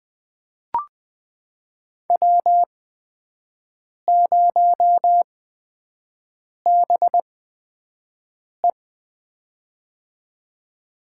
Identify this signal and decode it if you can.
Morse code, heard as a keyed tone, about 20 wpm, spaced as for 5 wpm, 700 Hz. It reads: W0BE
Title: USA Call Signs - 4 Characters - 8x Character Spacing - 20wpm